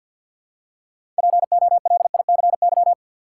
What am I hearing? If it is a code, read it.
Morse code, heard as a keyed tone, about 53 wpm, 700 Hz. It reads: POLICY
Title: Six Letter Words - 50wpm